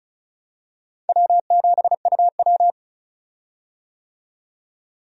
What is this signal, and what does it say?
Morse code, heard as a keyed tone, about 35 wpm, 700 Hz. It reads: W7UW
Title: USA Call Signs - 4 Characters - 35wpm